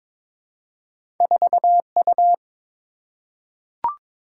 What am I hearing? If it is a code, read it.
Morse code, heard as a keyed tone, about 23 wpm, 700 Hz. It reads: 4U
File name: practice / 22wpm / Two Letter-Number Combinations - 22wpm